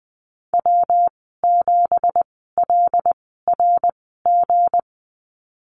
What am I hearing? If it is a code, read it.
Morse code, heard as a keyed tone, about 19 wpm, spaced as for 14 wpm, 700 Hz. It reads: W7LRG
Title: 200 Lesson 32g - Letter k - Call Signs 20wpm